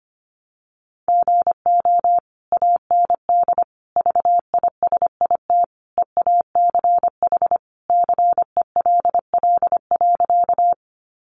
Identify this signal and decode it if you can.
Morse code, heard as a keyed tone, about 25 wpm, 700 Hz. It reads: GO AND VISIT EACH CELL.
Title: Sentences from Top 1100 Words 25wpm